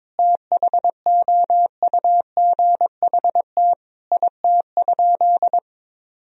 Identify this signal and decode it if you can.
Morse code, heard as a keyed tone, about 22 wpm, 700 Hz. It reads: THOUGHT IT?
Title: Sentences from Top 300 Words and Abbreviations 22wpm